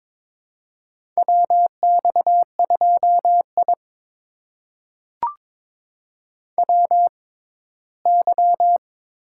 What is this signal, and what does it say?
Morse code, heard as a keyed tone, about 22 wpm, spaced as for 15 wpm, 700 Hz. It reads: WX2I WY
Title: USA Call Signs - 4 Characters - 6x Character Spacing - 22wpm